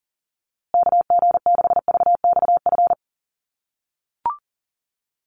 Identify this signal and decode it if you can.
Morse code, heard as a keyed tone, about 40 wpm, 700 Hz. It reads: KG6VXF
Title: USA Call Signs Repeated 2x 40wpm